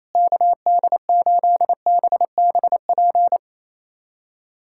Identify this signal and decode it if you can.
Morse code, heard as a keyed tone, about 28 wpm, 700 Hz. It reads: KD8BBP
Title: USA Callsign DE Callsign 28wpm